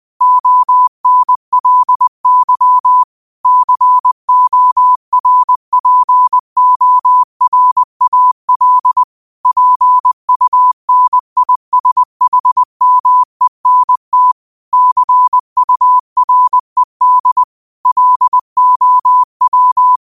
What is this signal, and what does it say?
Morse code, heard as a keyed tone, about 20 wpm, 1000 Hz. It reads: ONLY CORPORAL PUNISHMENT CURED LOW
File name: qotdfiles / 20wpm